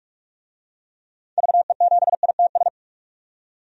Morse code, heard as a keyed tone, about 46 wpm, 700 Hz. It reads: VE7ITS
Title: DXCC-100 Call Signs - 6 Characters - 45wpm